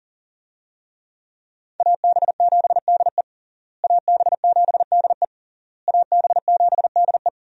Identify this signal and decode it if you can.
Morse code, heard as a keyed tone, about 40 wpm, 700 Hz. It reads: AB7DE AB7DE AB7DE
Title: USA Call Signs - 5 Characters - Repeated 3x - 40wpm